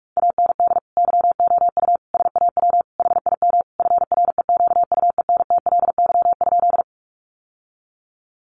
Morse code, heard as a keyed tone, about 44 wpm, 700 Hz. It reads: AND YOU SAW HIM FREQUENTLY?
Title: Sentences from Top 1300 Words 45wpm